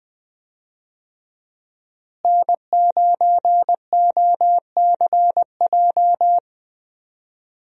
Morse code, heard as a keyed tone, about 20 wpm, 700 Hz. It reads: N9OCJ
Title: USA Call Signs - 5 Characters - Repeated 3x - 20wpm